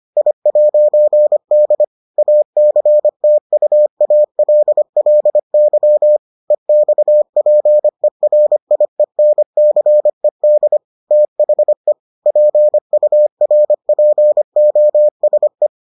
Morse code, heard as a keyed tone, about 25 wpm, 600 Hz. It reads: I'D ACTUALLY EXPERIENCED THE PURPOSE